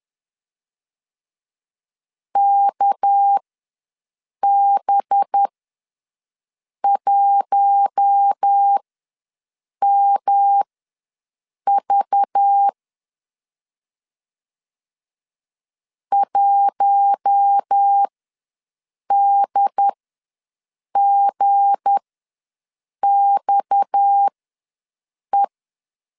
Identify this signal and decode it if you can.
Morse code, heard as a keyed tone, about 10 wpm, spaced as for 6 wpm, 800 Hz. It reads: KB1MV 1DGXE
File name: LEZIONI / 7 GF9Z / 7 G ws 10wpm 3843